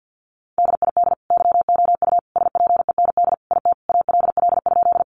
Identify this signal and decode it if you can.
Morse code, heard as a keyed tone, about 50 wpm, 700 Hz. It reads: DID YOU SPEND IT ALL?
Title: Sentences from Top 800 Words 50wpm